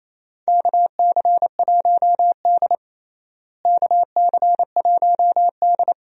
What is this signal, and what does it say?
Morse code, heard as a keyed tone, about 28 wpm, 700 Hz. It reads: KC1D KC1D
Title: USA Call Signs - 4 Characters - Repeated 2x - 28wpm